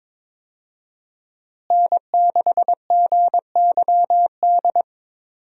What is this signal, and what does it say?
Morse code, heard as a keyed tone, about 22 wpm, 700 Hz. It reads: N6GYD
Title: USA Call Signs - 5 Characters - Repeated 2x - 22wpm